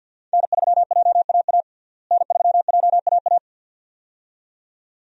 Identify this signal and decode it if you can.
Morse code, heard as a keyed tone, about 52 wpm, 700 Hz. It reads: N3JAA N3JAA
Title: USA Call Signs - 5 Characters - Repeated 2x - 50wpm